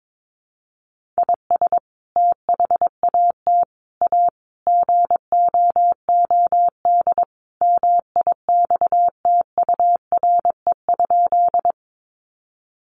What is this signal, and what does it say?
Morse code, heard as a keyed tone, about 22 wpm, 700 Hz. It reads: IS THAT A GOOD MIXTURE?
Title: Sentences from Top 1500 Words 22wpm